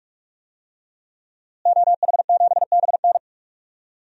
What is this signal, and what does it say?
Morse code, heard as a keyed tone, about 46 wpm, 700 Hz. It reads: OH7BN
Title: DXCC-100 Call Signs - 5 Characters - Repeated 2x - 45wpm